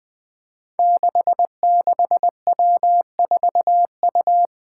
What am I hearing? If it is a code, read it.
Morse code, heard as a keyed tone, about 20 wpm, 700 Hz. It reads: 66W4U
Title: Five Challenging Letter-Number Combinations 20wpm